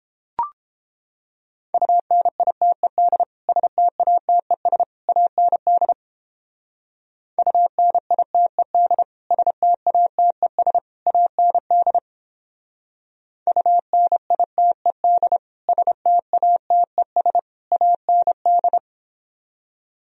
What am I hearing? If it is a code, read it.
Morse code, heard as a keyed tone, about 29 wpm, 700 Hz. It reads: UNITED STATES AND UNITED STATES AND UNITED STATES AND